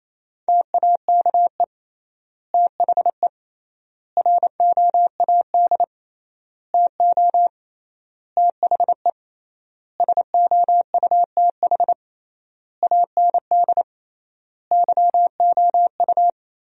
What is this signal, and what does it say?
Morse code, heard as a keyed tone, about 28 wpm, 700 Hz. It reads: TAKE THE ROAD TO THE SOUTH AND YOU
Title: 3x Word-Spacing - Sentences from Top 400 Words - 28wpm